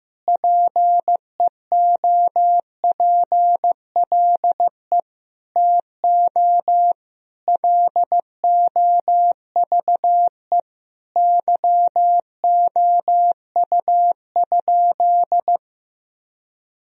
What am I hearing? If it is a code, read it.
Morse code, heard as a keyed tone, about 16 wpm, 700 Hz. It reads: PEOPLE TO LOVE YOU?
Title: Sentences from Top 1000 Words - Review Part 3 - 15wpm